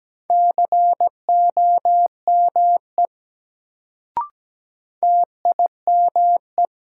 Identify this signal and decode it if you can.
Morse code, heard as a keyed tone, about 17 wpm, 700 Hz. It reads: COME TIME
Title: Sentences from Top 100 Words - New Word Drill - 17wpm